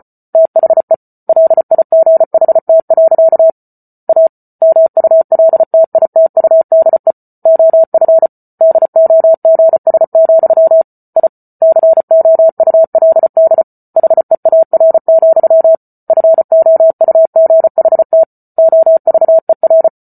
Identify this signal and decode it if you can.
Morse code, heard as a keyed tone, about 34 wpm, 650 Hz. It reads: THE LIGHT. A MULTITUDE OF DOGS, I COULD HEAR, FOUGHT OVER